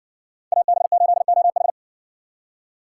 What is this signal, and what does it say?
Morse code, heard as a keyed tone, about 55 wpm, 700 Hz. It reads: AB8OH